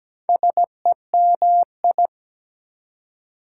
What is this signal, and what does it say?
Morse code, heard as a keyed tone, about 17 wpm, 700 Hz. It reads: SEMI
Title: Prefixes and Suffixes 17wpm